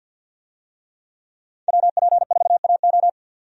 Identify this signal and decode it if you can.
Morse code, heard as a keyed tone, about 50 wpm, 700 Hz. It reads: WP4AO
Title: USA Call Signs - 5 Characters - 50wpm